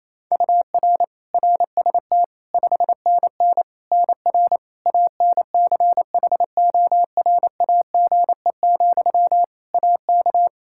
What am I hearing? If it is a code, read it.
Morse code, heard as a keyed tone, about 28 wpm, 700 Hz. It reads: UR RST 5NN NR ANCHORAGE, AK
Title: Practice QSO - Exchange 1 - Beginning - 28wpm